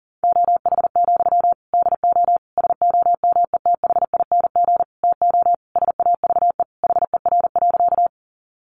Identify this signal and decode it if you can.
Morse code, heard as a keyed tone, about 40 wpm, 700 Hz. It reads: OH, DO SOMETHING TO SAVE HER.
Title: Sentences from Top 1100 Words 40wpm